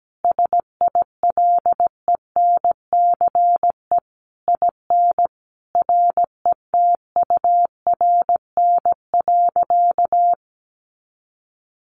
Morse code, heard as a keyed tone, about 17 wpm, 700 Hz. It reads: SILENCE IN RETURN.
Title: A Sound Of Thunder 17wpm